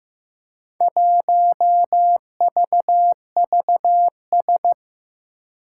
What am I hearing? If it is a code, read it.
Morse code, heard as a keyed tone, about 15 wpm, 700 Hz. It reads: 1VVS
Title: Four Challenging Letter-Number Combinations 15wpm